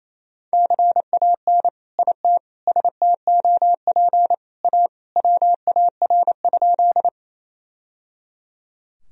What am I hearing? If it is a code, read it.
Morse code, heard as a keyed tone, about 28 wpm, 700 Hz. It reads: CAN IT STOP A WAR?
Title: Sentences from Top 1000 Words - Review Part 2 - 28wpm